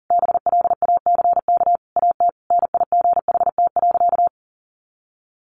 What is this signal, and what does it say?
Morse code, heard as a keyed tone, about 37 wpm, 700 Hz. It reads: BLACK AT NIGHT.
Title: Sentences from Top 1300 Words 40wpm